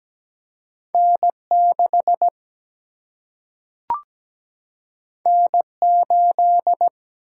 Morse code, heard as a keyed tone, about 17 wpm, 700 Hz. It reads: N6 N8
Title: Single-Letter Single-Number 17wpm